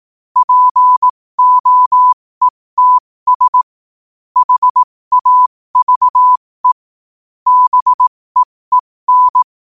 Morse code, heard as a keyed tone, about 18 wpm, spaced as for 15 wpm, 1000 Hz. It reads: POETS HAVE BEEN
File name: qotdfiles / 15wpm